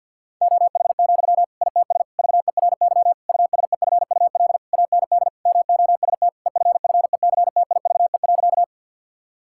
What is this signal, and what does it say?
Morse code, heard as a keyed tone, about 52 wpm, 700 Hz. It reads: OH, ITS VERY USEFUL AND MOST EFFECTIVE.
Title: Sentences from Top 1000 Words - Review Part 6 - 50wpm